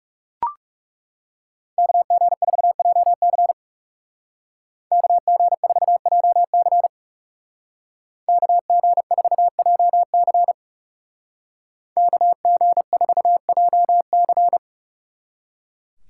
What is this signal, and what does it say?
Morse code, heard as a keyed tone, about 36 wpm, 700 Hz. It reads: KG4JC KG4JC KG4JC KG4JC
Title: Speed Racing - USA 2x2 Call Signs 30wpm